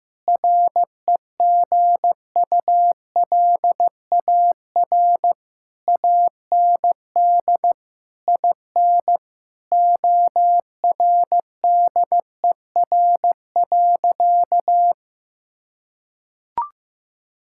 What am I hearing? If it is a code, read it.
Morse code, heard as a keyed tone, about 15 wpm, 700 Hz. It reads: REGULAR AND IN ORDER.
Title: Sentences from Top 1000 Words - Review Part 6 - 15wpm